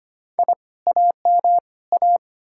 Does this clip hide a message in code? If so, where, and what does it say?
Yes: Morse code from 0.4 to 2.2 s, heard as a keyed tone, about 33 wpm, 700 Hz. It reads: I AM A